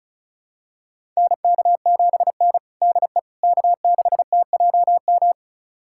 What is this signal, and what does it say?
Morse code, heard as a keyed tone, about 36 wpm, 700 Hz. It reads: NK7N DE K6TJM
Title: USA Callsign DE Callsign 35wpm